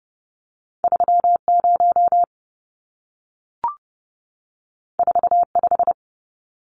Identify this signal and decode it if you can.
Morse code, heard as a keyed tone, about 30 wpm, 700 Hz. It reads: 30 45
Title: Numbers - 2-Digit 30wpm